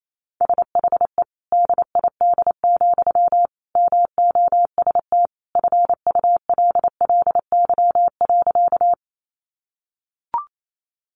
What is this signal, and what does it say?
Morse code, heard as a keyed tone, about 28 wpm, 700 Hz. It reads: SHE DID, MOST FULLY.
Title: Sentences from Top 1300 Words 28wpm